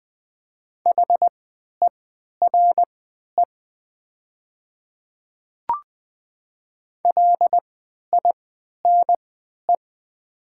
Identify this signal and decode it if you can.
Morse code, heard as a keyed tone, about 20 wpm, spaced as for 11 wpm, 700 Hz. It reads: HERE LINE
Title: 3x Character Spacing - Top 300 Words - 20wpm